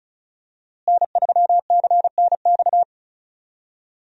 Morse code, heard as a keyed tone, about 35 wpm, 700 Hz. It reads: N3CNX
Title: USA Call Signs - 5 Characters - Repeated 2x - 35wpm